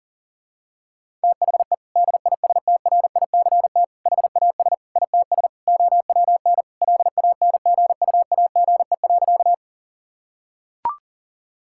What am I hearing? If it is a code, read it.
Morse code, heard as a keyed tone, about 40 wpm, 700 Hz. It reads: THE DISTRICT HAS ITS OWN LANGUAGE.
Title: Sentences from Top 1600 Words 40wpm